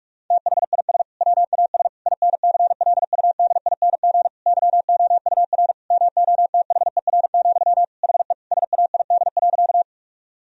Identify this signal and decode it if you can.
Morse code, heard as a keyed tone, about 46 wpm, 700 Hz. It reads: THIS WAS INCLUDING YOUR MOTHER, HE SAID.
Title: Sentences from Top 1000 Words - Review Part 4 - 45wpm